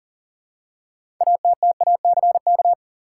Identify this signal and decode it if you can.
Morse code, heard as a keyed tone, about 40 wpm, 700 Hz. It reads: ATTACK
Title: Six Letter Words - 40wpm